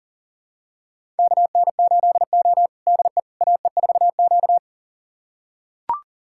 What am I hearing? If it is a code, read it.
Morse code, heard as a keyed tone, about 40 wpm, 700 Hz. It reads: KN8O DE AE4Q
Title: USA Callsign DE Callsign - 4 Characters - 40wpm